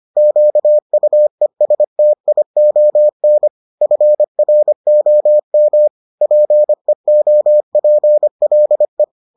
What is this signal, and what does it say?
Morse code, heard as a keyed tone, about 25 wpm, 600 Hz. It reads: QUESTION FROM PEOPLE